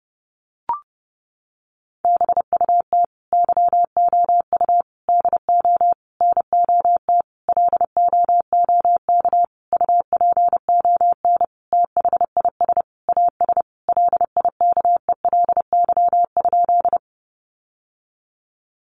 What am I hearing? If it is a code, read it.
Morse code, heard as a keyed tone, about 30 wpm, 700 Hz. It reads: BUT YOU DO NOT LOOK UPON THIS AS LIKELY?
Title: Sentences from Top 1100 Words 30wpm